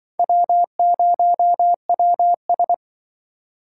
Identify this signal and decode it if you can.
Morse code, heard as a keyed tone, about 25 wpm, 700 Hz. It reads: W0WS